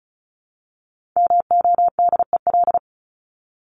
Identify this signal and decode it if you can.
Morse code, heard as a keyed tone, about 34 wpm, 700 Hz. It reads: MODEL
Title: Top 501-750 Words 35wpm